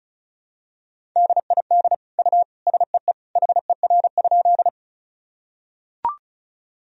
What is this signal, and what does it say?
Morse code, heard as a keyed tone, about 36 wpm, 700 Hz. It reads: DID U SEE HER?
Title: Sentences from Top 100 Words and Abbreviations 35wpm